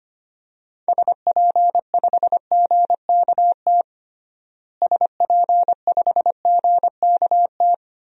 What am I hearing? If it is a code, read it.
Morse code, heard as a keyed tone, about 25 wpm, 700 Hz. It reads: SP5GKT SP5GKT
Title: DXCC-100 Call Signs - 6 Characters - Repeated 3x - 25wpm